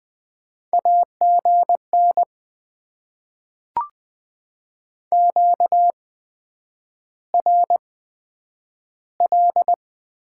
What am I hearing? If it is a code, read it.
Morse code, heard as a keyed tone, about 20 wpm, spaced as for 9 wpm, 700 Hz. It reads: AGN QRL
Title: Top 20 QSO Elements - 8x Character Spacing - 20wpm